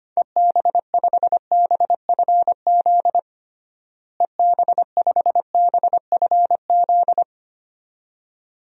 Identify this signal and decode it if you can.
Morse code, heard as a keyed tone, about 25 wpm, 700 Hz. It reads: EB5BFZ EB5BFZ
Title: DXCC-100 Call Signs - 6 Characters - Repeated 2x - 25wpm